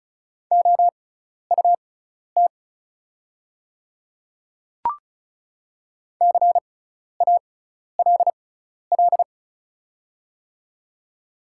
Morse code, heard as a keyed tone, about 36 wpm, spaced as for 12 wpm, 700 Hz. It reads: OUT CALL
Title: 6x Character Spacing - Top 200 Words - 35wpm